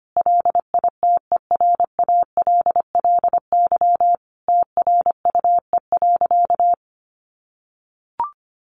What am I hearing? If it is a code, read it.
Morse code, heard as a keyed tone, about 25 wpm, 700 Hz. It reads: LITERALLY TRUE.
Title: Sentences from Top 1400 Words 25wpm